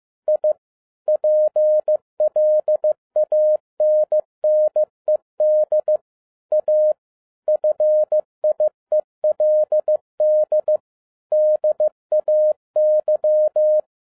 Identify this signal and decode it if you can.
Morse code, heard as a keyed tone, about 15 wpm, 610 Hz. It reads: I PLANNED A FIELD DAY
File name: FIELD DAY CONTEST QRV FROM SAKAE VILLAGE 15wpm